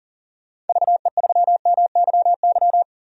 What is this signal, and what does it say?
Morse code, heard as a keyed tone, about 40 wpm, 700 Hz. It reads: VE3MYY
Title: DXCC-100 Call Signs - 6 Characters - 40wpm